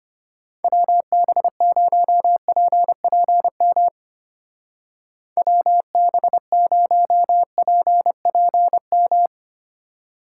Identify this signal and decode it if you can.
Morse code, heard as a keyed tone, about 31 wpm, 700 Hz. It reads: WB0PPM WB0PPM
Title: Speed Racing - USA Call Signs 25wpm